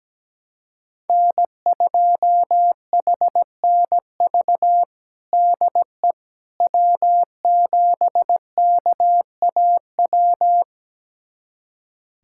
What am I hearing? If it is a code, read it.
Morse code, heard as a keyed tone, about 17 wpm, 700 Hz. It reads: N2HNV DE W7KAW
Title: USA Callsign DE Callsign - 5 Characters - 17wpm